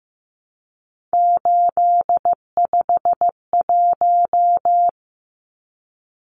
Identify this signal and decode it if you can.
Morse code, heard as a keyed tone, about 15 wpm, 700 Hz. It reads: 851
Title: Numbers - 3-Digit 15wpm